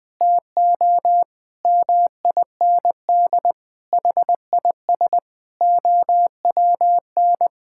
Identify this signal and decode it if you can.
Morse code, heard as a keyed tone, about 21 wpm, 700 Hz. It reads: TO MIND HIS OWN
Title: Sentences from Top 1000 Words - Review Part 6 - 20wpm